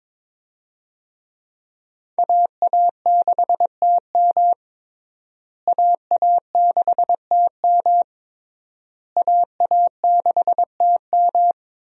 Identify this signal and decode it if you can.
Morse code, heard as a keyed tone, about 21 wpm, 700 Hz. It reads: AA6TM AA6TM AA6TM
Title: USA Call Signs - 5 Characters - Repeated 3x - 22wpm